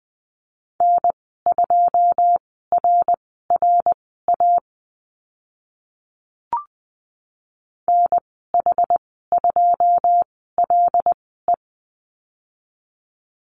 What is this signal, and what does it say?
Morse code, heard as a keyed tone, about 20 wpm, spaced as for 14 wpm, 700 Hz. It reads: N2RRA NH2LE
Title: 082 Lesson 14g - Number 2 - Call Signs 20wpm